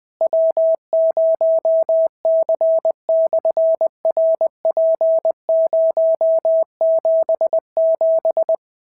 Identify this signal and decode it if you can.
Morse code, heard as a keyed tone, about 20 wpm, 690 Hz. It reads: W0C/RP077
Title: SOTA QSO Practice 20wpm